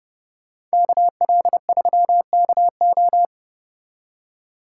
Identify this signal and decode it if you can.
Morse code, heard as a keyed tone, about 34 wpm, 700 Hz. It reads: KL3KO